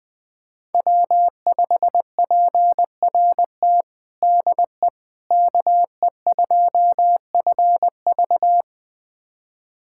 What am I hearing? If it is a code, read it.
Morse code, heard as a keyed tone, about 20 wpm, 700 Hz. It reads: W5PRT DE KE2FV
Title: USA Callsign DE Callsign - 5 Characters - 20wpm